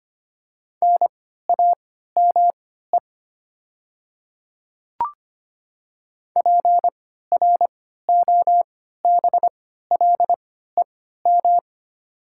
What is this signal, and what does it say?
Morse code, heard as a keyed tone, about 25 wpm, spaced as for 14 wpm, 690 Hz. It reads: NAME PROBLEM
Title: 3x Character Spacing - Top 500 Words - 25wpm